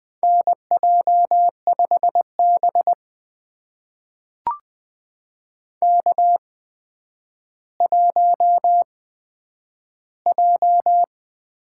Morse code, heard as a keyed tone, about 20 wpm, spaced as for 10 wpm, 700 Hz. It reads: NJ5B K1J
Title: USA Call Signs - 4 Characters - 8x Character Spacing - 20wpm